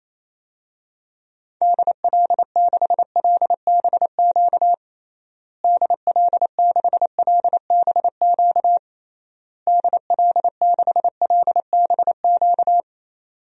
Morse code, heard as a keyed tone, about 27 wpm, 700 Hz. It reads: DL6LBQ DL6LBQ DL6LBQ
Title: DXCC-100 Call Signs - 6 Characters - Repeated 3x - 28wpm